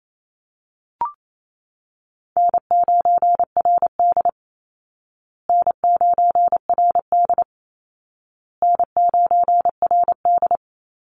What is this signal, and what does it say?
Morse code, heard as a keyed tone, about 28 wpm, 700 Hz. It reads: N9RD N9RD N9RD
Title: Easy Call Signs Repeated 3x - 28wpm